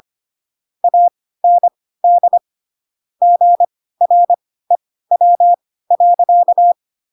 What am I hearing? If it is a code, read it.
Morse code, heard as a keyed tone, about 25 wpm, spaced as for 16 wpm, 700 Hz. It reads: AND GREW.
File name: Pinocchio / Pinocchio ch5 10 25 ik1hge